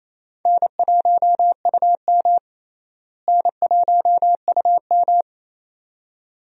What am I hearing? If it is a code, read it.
Morse code, heard as a keyed tone, about 28 wpm, 700 Hz. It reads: N1UM N1UM
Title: USA Call Signs - 4 Characters - Repeated 2x - 28wpm